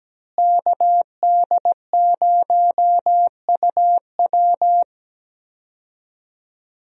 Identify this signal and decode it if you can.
Morse code, heard as a keyed tone, about 17 wpm, 700 Hz. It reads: KD0UW
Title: USA Call Signs - 5 Characters - Repeated 2x - 17wpm